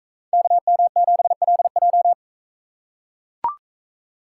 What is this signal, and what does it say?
Morse code, heard as a keyed tone, about 40 wpm, 700 Hz. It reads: KM7LJ